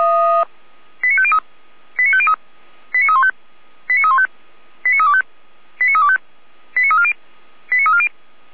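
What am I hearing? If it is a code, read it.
Five-tone selective call (ZVEI), heard as quick runs of five tones, tones of about 70 ms. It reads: 89503 89503 89315 89315 89325 89325 89340 89340
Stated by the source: ZVEI five-tone selective call received by SDR, hosted by Feuerwehr Orlamünde fire brigade.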